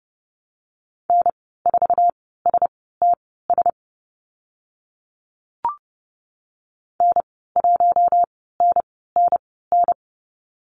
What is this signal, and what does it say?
Morse code, heard as a keyed tone, about 30 wpm, spaced as for 17 wpm, 700 Hz. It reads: N4STS N1NNN
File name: Lesson 14g - Number 2 - Call Signs 30wpm